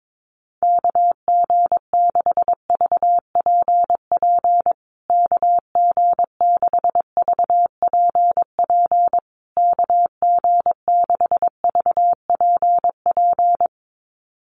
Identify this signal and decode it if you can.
Morse code, heard as a keyed tone, about 22 wpm, 700 Hz. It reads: KG6VPP KG6VPP KG6VPP
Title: USA Call Signs Repeated 3x 22wpm